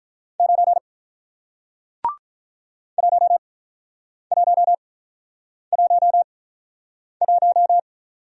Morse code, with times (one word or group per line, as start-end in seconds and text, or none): 0.4-0.8 9
3.0-7.8 1111